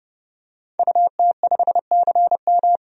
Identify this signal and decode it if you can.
Morse code, heard as a keyed tone, about 30 wpm, 700 Hz. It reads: UT5CM
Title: DXCC-100 Call Signs - 5 Characters - 30wpm